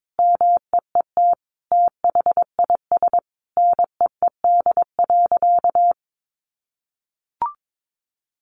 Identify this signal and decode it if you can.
Morse code, heard as a keyed tone, about 22 wpm, 700 Hz. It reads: MEET THIS NEED.
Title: Sentences from Top 1100 Words 22wpm